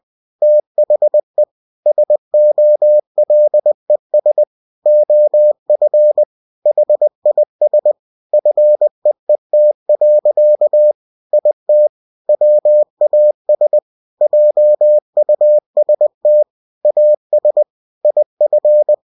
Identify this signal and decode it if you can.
Morse code, heard as a keyed tone, about 20 wpm, 600 Hz. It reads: THE SOLES OF HIS FEET. IT WAS JUST AS IF
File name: cw-books / winnie-the-pooh / winnie-20wpm-page-0049